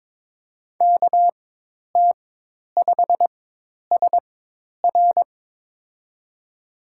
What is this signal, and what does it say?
Morse code, heard as a keyed tone, about 22 wpm, spaced as for 10 wpm, 700 Hz. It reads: KT5SR
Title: USA Call Signs - 5 Characters - 4x Character Spacing - 22wpm